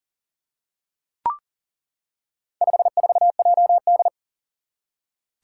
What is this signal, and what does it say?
Morse code, heard as a keyed tone, about 40 wpm, 700 Hz. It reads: 54JD
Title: Four Challenging Letter-Number Combinations 40wpm